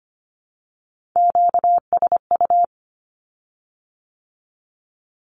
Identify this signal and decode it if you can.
Morse code, heard as a keyed tone, about 25 wpm, 700 Hz. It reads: QSU
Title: Q-Codes with Spoken Meaning - 25wpm